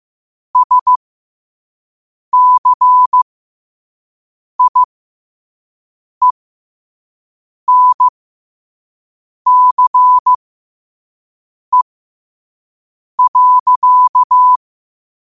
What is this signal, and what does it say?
Morse code, heard as a keyed tone, about 15 wpm, spaced as for 5 wpm, 1000 Hz. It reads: SCIENCE.
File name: qotdfiles / 5wpm